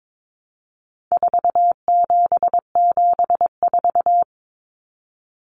Morse code, heard as a keyed tone, about 22 wpm, 700 Hz. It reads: 4774
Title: Common Cut Numbers - 4-Digit 22wpm